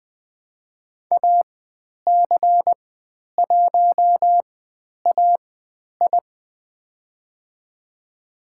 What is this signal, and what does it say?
Morse code, heard as a keyed tone, about 20 wpm, spaced as for 10 wpm, 700 Hz. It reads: AC1AI
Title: 102 Lesson 17g - Letter u - Call Signs 20wpm